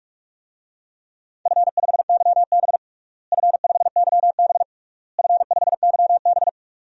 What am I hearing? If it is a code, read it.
Morse code, heard as a keyed tone, about 41 wpm, 700 Hz. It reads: F5YB F5YB F5YB
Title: DXCC-100 Call Signs - 4 Characters - Repeated 3x - 45wpm